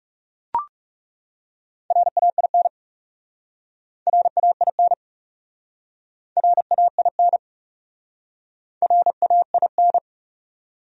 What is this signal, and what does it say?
Morse code, heard as a keyed tone, about 40 wpm, 700 Hz. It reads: RAIN RAIN RAIN RAIN